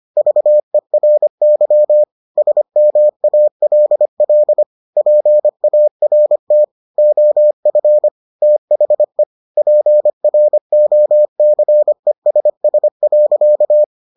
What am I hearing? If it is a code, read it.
Morse code, heard as a keyed tone, about 25 wpm, 600 Hz. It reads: VERY SMALL PART OF THE PROCESS.